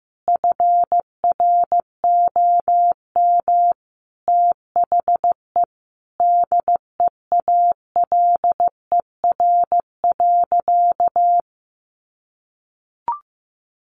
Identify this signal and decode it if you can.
Morse code, heard as a keyed tone, about 15 wpm, 700 Hz. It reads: FROM THE DEALER.
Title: Sentences from Top 1100 Words 15wpm